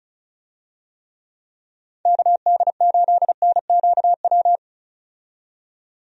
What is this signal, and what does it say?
Morse code, heard as a keyed tone, about 36 wpm, 700 Hz. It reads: KD8NQW
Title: USA Call Signs - 6 Characters - 35wpm